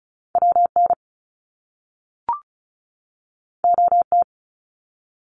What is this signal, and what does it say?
Morse code, heard as a keyed tone, about 36 wpm, 700 Hz. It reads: WN OT
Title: Top Two Letter Combinations in Words 35wpm